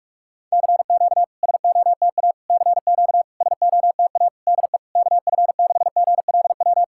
Morse code, heard as a keyed tone, about 48 wpm, 700 Hz. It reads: CQ SOTA CQ SOTA DE KF6GLW
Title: SOTA QSO Practice 45wpm